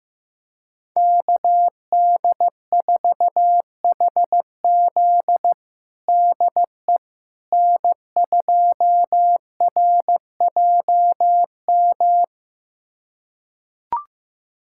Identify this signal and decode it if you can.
Morse code, heard as a keyed tone, about 15 wpm, 700 Hz. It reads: KD4HZ DE N2RJM
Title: USA Callsign DE Callsign - 5 Characters - 15wpm